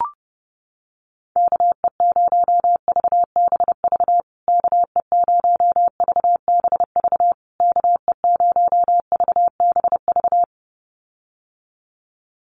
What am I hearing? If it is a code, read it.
Morse code, heard as a keyed tone, about 30 wpm, 700 Hz. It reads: KE0VBV KE0VBV KE0VBV
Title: USA Call Signs Repeated 3x 30wpm